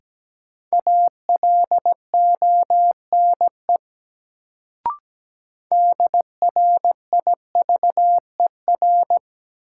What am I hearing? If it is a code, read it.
Morse code, heard as a keyed tone, about 17 wpm, 700 Hz. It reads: ALONE DRIVER
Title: Sentences from Top 1000 Words - New Word Drill - 17wpm